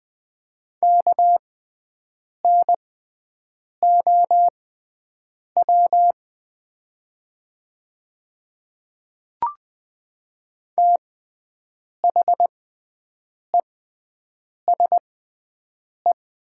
Morse code, heard as a keyed tone, about 20 wpm, spaced as for 7 wpm, 700 Hz. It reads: KNOW THESE
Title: 6x Character Spacing - Top 500 Words - 20wpm